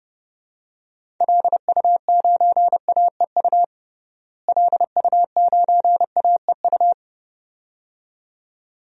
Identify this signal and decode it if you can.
Morse code, heard as a keyed tone, about 30 wpm, 700 Hz. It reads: LU9AEU LU9AEU
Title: DXCC-100 Call Signs - 6 Characters - Repeated 2x - 30wpm